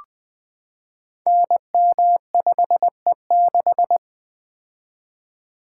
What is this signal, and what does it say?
Morse code, heard as a keyed tone, about 20 wpm, 700 Hz. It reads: NM5E6
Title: Code Groups - 5 Letter:Numbers - 20wpm